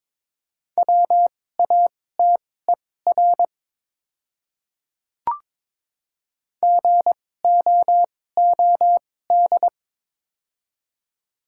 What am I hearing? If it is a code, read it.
Morse code, heard as a keyed tone, about 22 wpm, spaced as for 16 wpm, 700 Hz. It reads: WATER GOOD
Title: 2x Character Spacing - Top 200 Words - 22wpm